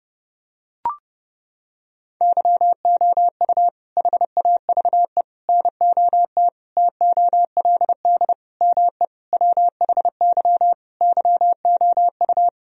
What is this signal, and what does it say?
Morse code, heard as a keyed tone, about 32 wpm, 700 Hz. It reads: YOU HAVE NOT TOLD ME WHY YOU
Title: Sentences from Top 1000 Words - Review Part 2 - 30wpm